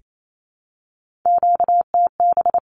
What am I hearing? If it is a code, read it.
Morse code, heard as a keyed tone, about 28 wpm, 700 Hz. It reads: QTB
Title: Q-Codes Repeated 3x 28wpm